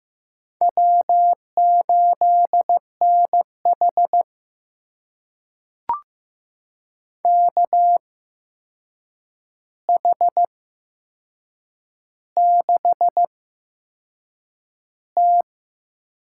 Morse code, heard as a keyed tone, about 16 wpm, spaced as for 6 wpm, 700 Hz. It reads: W8NH KH6T
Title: USA Call Signs - 4 Characters - 8x Character Spacing - 15wpm